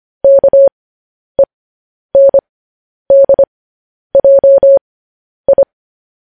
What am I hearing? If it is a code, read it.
Morse code, heard as a keyed tone, about 25 wpm, spaced as for 10 wpm, 550 Hz. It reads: KENDJI